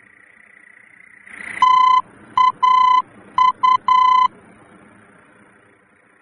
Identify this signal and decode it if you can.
Morse code, heard as a keyed tone, about 9 wpm, 1020 Hz. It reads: TAU
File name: germany / mx / TAU-D-VOR